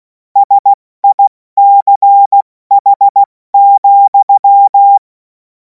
Morse code, heard as a keyed tone, about 16 wpm, 800 Hz. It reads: SICH,
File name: Morsekurs / Witze / Witz 07. April 2026 70 F80